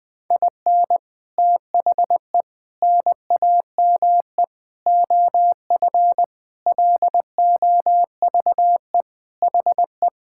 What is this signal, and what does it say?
Morse code, heard as a keyed tone, about 20 wpm, 700 Hz. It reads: IN THE NAME OF LOVE HE
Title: Sentences with 4-Word Phrases - Spoken Beginning - 20wpm